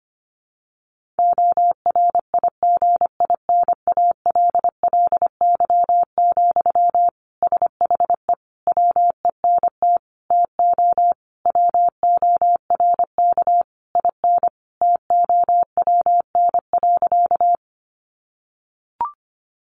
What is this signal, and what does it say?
Morse code, heard as a keyed tone, about 25 wpm, 700 Hz. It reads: ORIGINALLY, SHE WENT TO WORK IN TOWN.
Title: Sentences from Top 1500 Words 25wpm